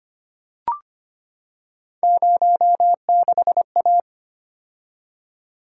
Morse code, heard as a keyed tone, about 25 wpm, 700 Hz. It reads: 06A
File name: Three Letter-Number Combinations - 25wpm